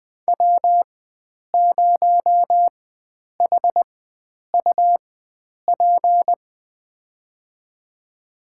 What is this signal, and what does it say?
Morse code, heard as a keyed tone, about 20 wpm, spaced as for 9 wpm, 700 Hz. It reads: W0HUP
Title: USA Call Signs - 4x Character Spacing - 20wpm